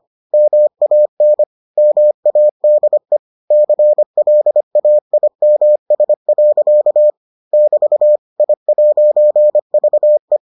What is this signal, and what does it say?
Morse code, heard as a keyed tone, about 25 wpm, 600 Hz. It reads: MAN MADE CLAIMS. <BT> I'VE